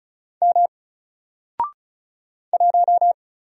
Morse code, heard as a keyed tone, about 36 wpm, 700 Hz. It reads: M 1
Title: Single Letter-Number 35wpm